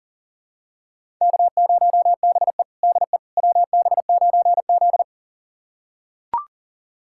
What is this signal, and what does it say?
Morse code, heard as a keyed tone, about 40 wpm, 690 Hz. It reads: K0BE DE WB9Z
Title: USA Callsign DE Callsign - 4 Characters - 40wpm